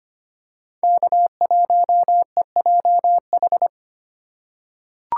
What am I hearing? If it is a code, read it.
Morse code, heard as a keyed tone, about 25 wpm, 700 Hz. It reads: K1EJH
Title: USA Call Signs - 3x Character Spacing - 25wpm